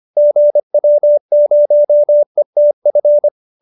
Morse code, heard as a keyed tone, about 25 wpm, 590 Hz. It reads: GW0ETF
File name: QSO318 25